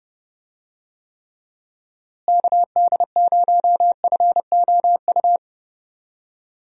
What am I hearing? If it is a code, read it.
Morse code, heard as a keyed tone, about 30 wpm, 700 Hz. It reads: KD0FOU
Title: USA Call Signs - 6 Characters - Repeated 2x - 30wpm